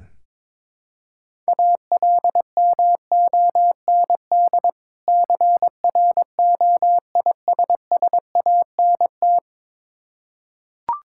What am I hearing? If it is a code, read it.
Morse code, heard as a keyed tone, about 22 wpm, 690 Hz. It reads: ALMOND CROISSANT
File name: Difficult Words with Hints - 22wpm